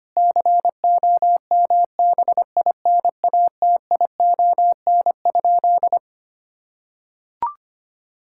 Morse code, heard as a keyed tone, about 25 wpm, 700 Hz. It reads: COMBINATION?
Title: Sentences from Top 1000 Words - Review Part 6 - 25wpm